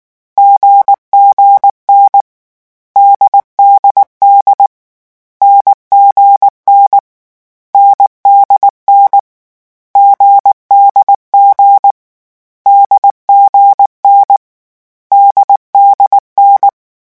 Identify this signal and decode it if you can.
Morse code, heard as a keyed tone, about 19 wpm, 800 Hz. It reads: GGN DDD NGN NDN GDG DGN DDN
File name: morse mp3 / lecon4-vitesse 18